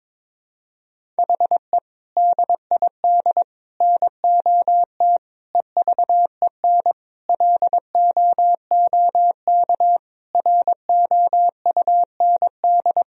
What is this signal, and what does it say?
Morse code, heard as a keyed tone, about 22 wpm, 700 Hz. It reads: HE DID NOT EVEN LOOK ROUND
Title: Sentences from Top 1000 Words - Review Part 1 - 22wpm